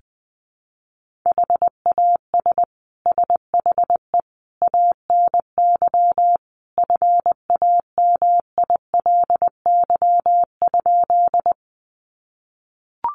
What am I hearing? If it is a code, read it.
Morse code, heard as a keyed tone, about 20 wpm, 700 Hz. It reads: HAS SHE ANY FAMILY?
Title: Sentences from Top 400 Words 20wpm